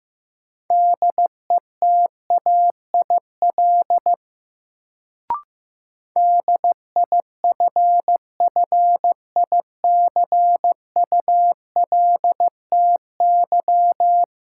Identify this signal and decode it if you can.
Morse code, heard as a keyed tone, about 15 wpm, 700 Hz. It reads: DETAIL DIFFICULTY
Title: Sentences from Top 1500 Words - New Word Drill - 15wpm